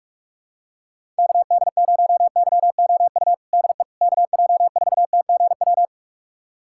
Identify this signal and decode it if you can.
Morse code, heard as a keyed tone, about 46 wpm, 700 Hz. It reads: KD0YOU DE KJ4TGW
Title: USA Callsign DE Callsign - 6 Characters - 45wpm